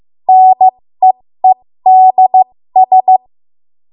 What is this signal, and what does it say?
Morse code, heard as a keyed tone, about 15 wpm, spaced as for 13 wpm, 750 Hz. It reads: NEEDS
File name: Morse / 13WPM